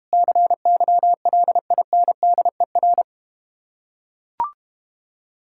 Morse code, heard as a keyed tone, about 32 wpm, 700 Hz. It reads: CYLINDER